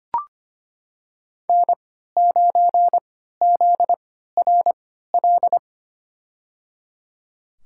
Morse code, heard as a keyed tone, about 25 wpm, spaced as for 14 wpm, 700 Hz. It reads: N9ZRL
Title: USA Call Signs - 5 Characters - 3x Character Spacing - 25wpm